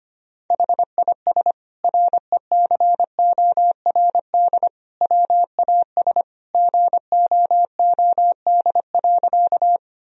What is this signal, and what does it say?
Morse code, heard as a keyed tone, about 25 wpm, 700 Hz. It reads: HIS RECORD WAS GOOD.
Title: Sentences from Top 1000 Words - Review Part 3 - 25wpm